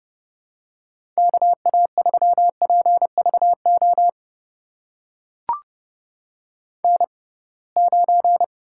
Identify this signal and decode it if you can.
Morse code, heard as a keyed tone, about 30 wpm, spaced as for 23 wpm, 700 Hz. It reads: KA3PVO N9
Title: USA Call Signs - 6x Character Spacing - 30wpm